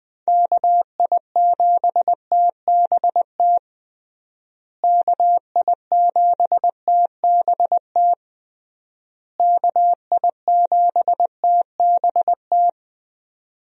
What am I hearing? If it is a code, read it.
Morse code, heard as a keyed tone, about 20 wpm, 700 Hz. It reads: KI7TBT KI7TBT KI7TBT
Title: USA Call Signs - 6 Characters - Repeated 3x - 20wpm